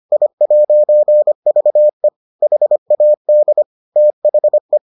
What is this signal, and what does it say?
Morse code, heard as a keyed tone, about 25 wpm, 600 Hz. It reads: I'VE HAD THE